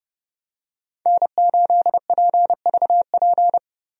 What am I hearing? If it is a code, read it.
Morse code, heard as a keyed tone, about 30 wpm, 700 Hz. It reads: N8PVP
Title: USA Call Signs - 5 Characters - Repeated 3x - 30wpm